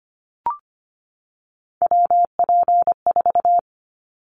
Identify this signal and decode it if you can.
Morse code, heard as a keyed tone, about 25 wpm, 700 Hz. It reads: WP4
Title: Callsign Prefixes for Top 25 DXCC - 25wpm